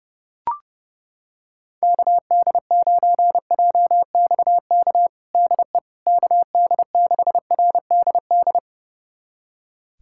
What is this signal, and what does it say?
Morse code, heard as a keyed tone, about 30 wpm, 700 Hz. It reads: KD9JXK DE KD6RDD
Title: USA Callsign DE Callsign - 6 Characters - 30wpm